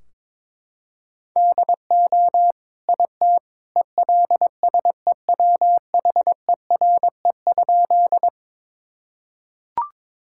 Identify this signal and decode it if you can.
Morse code, heard as a keyed tone, about 22 wpm, 700 Hz. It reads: DO IT ELSEWHERE?
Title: Sentences from Top 1600 Words 22wpm